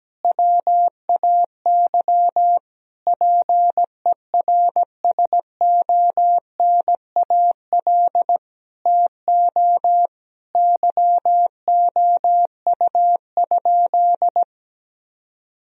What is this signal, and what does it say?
Morse code, heard as a keyed tone, about 17 wpm, 700 Hz. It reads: WAY PERSONAL TO YOU?
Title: Sentences from Top 1000 Words - Review Part 4 - 17wpm